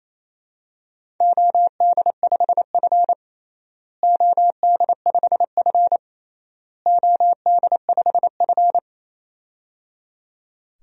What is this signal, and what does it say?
Morse code, heard as a keyed tone, about 28 wpm, 700 Hz. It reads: OD5F OD5F OD5F
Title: DXCC-100 Call Signs - 4 Characters - Repeated 3x - 28wpm